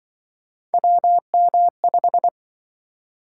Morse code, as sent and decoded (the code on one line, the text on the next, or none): .-- -- .....
WM5